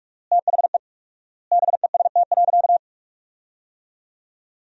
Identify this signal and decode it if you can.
Morse code, heard as a keyed tone, about 43 wpm, 690 Hz. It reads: THE BEST.
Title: 4x Word-Spacing - Sentences from Top 500 Words - 45wpm